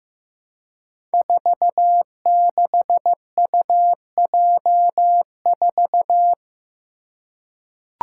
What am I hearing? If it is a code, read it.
Morse code, heard as a keyed tone, about 15 wpm, 700 Hz. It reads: 46UJ4
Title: Five Challenging Letter-Number Combinations 15wpm